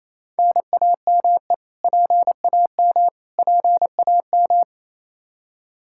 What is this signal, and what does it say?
Morse code, heard as a keyed tone, about 28 wpm, 700 Hz. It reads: NAME PAM PAM
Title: Practice QSO - Exchange 1 - Beginning - 28wpm